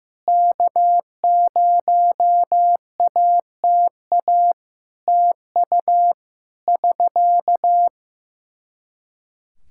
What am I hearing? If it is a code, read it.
Morse code, heard as a keyed tone, about 15 wpm, 700 Hz. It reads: K0ATA TU <SK>
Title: Practice QSO - Exchange 4 - Ending - 15wpm